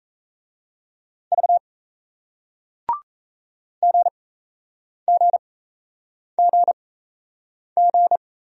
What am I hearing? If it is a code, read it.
Morse code, heard as a keyed tone, about 38 wpm, spaced as for 8 wpm, 700 Hz. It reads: V GGGG